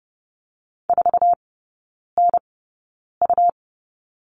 Morse code, heard as a keyed tone, about 30 wpm, spaced as for 9 wpm, 700 Hz. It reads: 4NU
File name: Lesson 17e - Letter u - Three Characters 30wpm